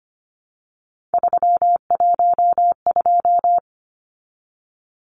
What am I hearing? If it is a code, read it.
Morse code, heard as a keyed tone, about 25 wpm, 700 Hz. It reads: 312
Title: Numbers - 3-Digit 25wpm